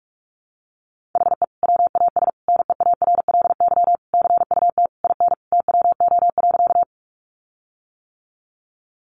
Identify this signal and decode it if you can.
Morse code, heard as a keyed tone, about 46 wpm, 700 Hz. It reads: HE WAS NEARLY CUT IN TWO.
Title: Sentences from Top 1300 Words 45wpm